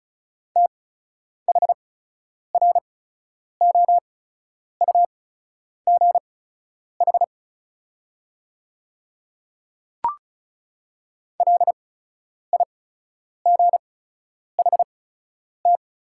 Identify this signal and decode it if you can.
Morse code, heard as a keyed tone, about 36 wpm, spaced as for 10 wpm, 700 Hz. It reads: THROUGH LIGHT
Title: 8x Character Spacing - Top 200 Words - 35wpm